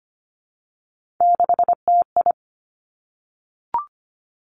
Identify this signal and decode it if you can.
Morse code, heard as a keyed tone, about 25 wpm, 700 Hz. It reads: 6TI
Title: Three Challenging Letter-Number Combinations 25wpm